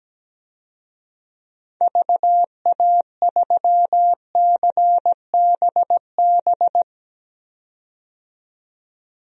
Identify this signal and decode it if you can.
Morse code, heard as a keyed tone, about 17 wpm, 700 Hz. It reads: VA3CBB
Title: DXCC-100 Call Signs - 6 Characters - 17wpm